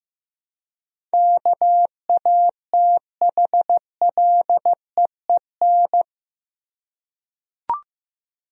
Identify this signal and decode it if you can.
Morse code, heard as a keyed tone, about 15 wpm, 700 Hz. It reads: KATHLEEN
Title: USA Common Names - 15wpm